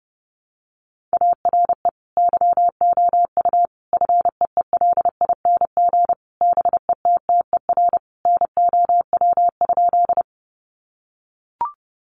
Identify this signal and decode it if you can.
Morse code, heard as a keyed tone, about 30 wpm, 700 Hz. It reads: ARE YOU FEELING BETTER NOW?
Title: Sentences from Top 1200 Words 30wpm